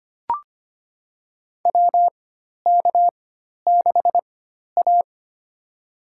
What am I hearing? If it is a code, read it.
Morse code, heard as a keyed tone, about 25 wpm, spaced as for 12 wpm, 700 Hz. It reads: WK6A
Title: USA Call Signs - 4x Character Spacing - 25wpm